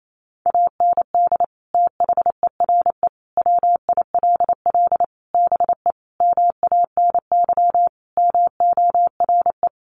Morse code, heard as a keyed tone, about 28 wpm, 700 Hz. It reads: AND THERE WILL BE MANY MORE
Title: Sentences from Top 1500 Words 28wpm